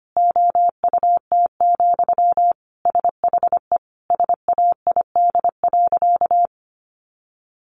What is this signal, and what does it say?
Morse code, heard as a keyed tone, about 25 wpm, 700 Hz. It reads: OUT, SHE SAID.
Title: Sentences from Top 500 Words 25wpm